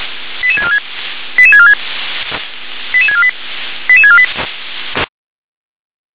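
Five-tone selective call (ZVEI), heard as quick runs of five tones, tones of about 70 ms. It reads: 88646 88646 88648 88648
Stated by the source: ZVEI five-tone selective call received on SDR.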